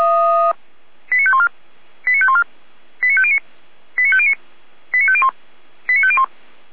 Five-tone selective call (ZVEI), heard as quick runs of five tones, tones of about 70 ms. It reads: 89514 89514 89509 89509 89501 89501